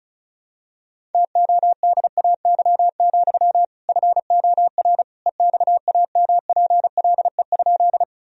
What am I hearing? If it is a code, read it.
Morse code, heard as a keyed tone, about 36 wpm, 700 Hz. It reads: TODAY, FOR EXAMPLE?
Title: Sentences from Top 1000 Words - Review Part 1 - 35wpm